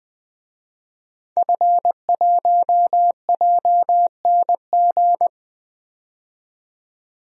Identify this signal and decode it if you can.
Morse code, heard as a keyed tone, about 20 wpm, 700 Hz. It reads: F1JNG
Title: DXCC-100 Call Signs - 5 Characters - 20wpm